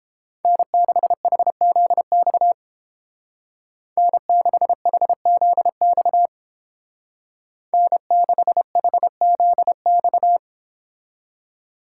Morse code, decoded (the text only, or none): N6HZX N6HZX N6HZX